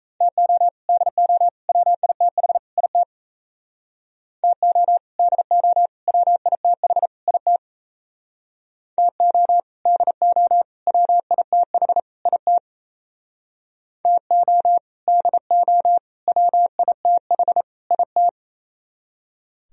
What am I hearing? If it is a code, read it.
Morse code, heard as a keyed tone, about 35 wpm, 700 Hz. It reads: TO DO WITH IT TO DO WITH IT TO DO WITH IT TO DO WITH IT
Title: Speed-Racing - Sets of 4 Words 28wpm